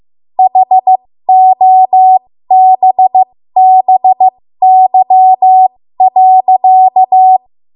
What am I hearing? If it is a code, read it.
Morse code, heard as a keyed tone, about 15 wpm, spaced as for 13 wpm, 750 Hz. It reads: HOBBY.
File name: Morse / 13WPM